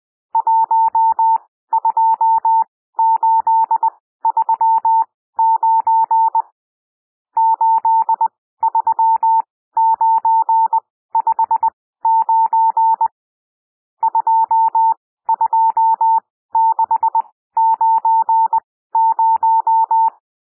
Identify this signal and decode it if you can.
Morse code, heard as a keyed tone, about 20 wpm, spaced as for 14 wpm, 900 Hz. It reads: 12839 83959 22690
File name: cwmp3 / N1 SOUND NUMBERS 20 wpm